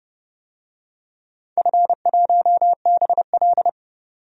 Morse code, heard as a keyed tone, about 30 wpm, 700 Hz. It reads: F1BL
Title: DXCC-100 Call Signs - 4 Characters - 30wpm